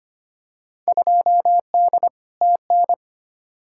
Morse code, heard as a keyed tone, about 25 wpm, 700 Hz. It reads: 2D TN
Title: Field Day Exchanges Encore - 25wpm